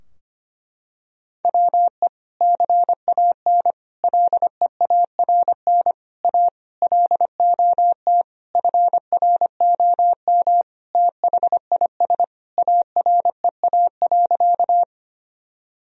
Morse code, heard as a keyed tone, about 25 wpm, 700 Hz. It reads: WE CAN LEARN A LOT FROM THIS AREA.
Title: Sentences from Top 1000 Words - Review Part 2 - 25wpm